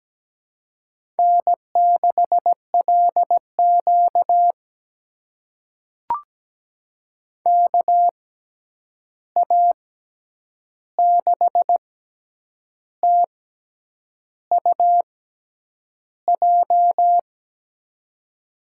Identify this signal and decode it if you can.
Morse code, heard as a keyed tone, about 17 wpm, spaced as for 8 wpm, 700 Hz. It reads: N6LQ KA6TUJ
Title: USA Call Signs - 6x Character Spacing - 17wpm